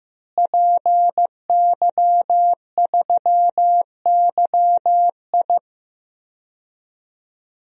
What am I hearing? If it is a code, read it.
Morse code, heard as a keyed tone, about 15 wpm, 700 Hz. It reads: PY3YI
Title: DXCC-100 Call Signs - 5 Characters - Repeated 3x - 15wpm